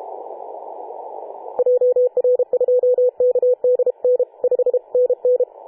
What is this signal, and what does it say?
Morse code, heard as a keyed tone, about 32 wpm, 500 Hz. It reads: JR2KDN 5NN
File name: 2013/Sep/21 2035z 3D2GC 18.072MHz CW